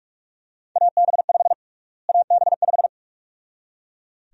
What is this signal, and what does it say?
Morse code, heard as a keyed tone, about 41 wpm, 700 Hz. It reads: AB5 AB5
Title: USA Call Sign Prefixes - Repeated 2x - 45wpm